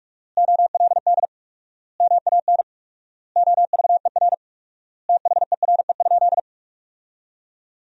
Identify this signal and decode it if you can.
Morse code, heard as a keyed tone, about 46 wpm, 700 Hz. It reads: OLD MAN OVER THERE?
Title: 4x Word-Spacing - Sentences from Top 200 Words - 45wpm